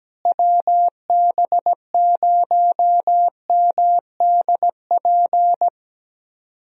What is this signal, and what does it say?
Morse code, heard as a keyed tone, about 17 wpm, 700 Hz. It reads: WB0MDP
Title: USA Call Signs - 6 Characters - Repeated 3x - 17wpm